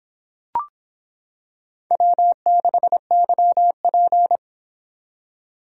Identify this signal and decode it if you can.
Morse code, heard as a keyed tone, about 25 wpm, 700 Hz. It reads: W6YP